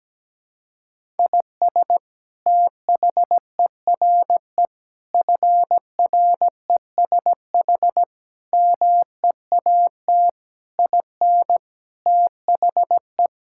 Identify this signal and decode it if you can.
Morse code, heard as a keyed tone, about 17 wpm, 700 Hz. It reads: IS THERE FRESH MEAT IN THE
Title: Sentences from Top 1600 Words 17wpm